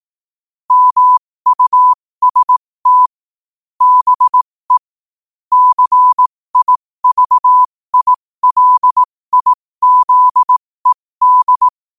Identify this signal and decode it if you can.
Morse code, heard as a keyed tone, about 18 wpm, spaced as for 15 wpm, 1000 Hz. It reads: MUST BE CIVILIZED
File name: qotdfiles / 15wpm